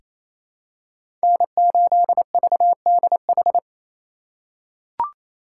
Morse code, heard as a keyed tone, about 28 wpm, 700 Hz. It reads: N8VDH
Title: USA Call Signs - 5 Characters - 4x Character Spacing - 28wpm